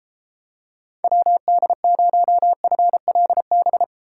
Morse code, heard as a keyed tone, about 31 wpm, 700 Hz. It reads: WD0FLB